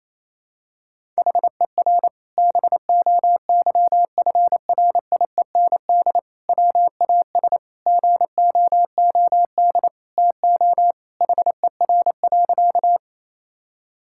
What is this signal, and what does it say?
Morse code, heard as a keyed tone, about 28 wpm, 700 Hz. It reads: HER BOYFRIEND WAS GOOD TO HER.
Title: Sentences from Top 1000 Words - Review Part 5 - 28wpm